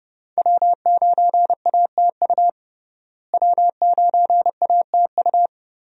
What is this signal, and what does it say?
Morse code, heard as a keyed tone, about 30 wpm, 700 Hz. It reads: W9ATU W9ATU
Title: USA Call Signs - 5 Characters - Repeated 3x - 30wpm